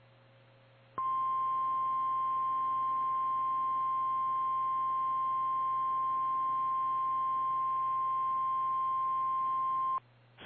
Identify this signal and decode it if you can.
Emergency Alert System (SAME) tones (the 1050 Hz attention tone). It is heard from 0.9 to 10.0 s.
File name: eas / KKPRKXPN / KKPRKXPN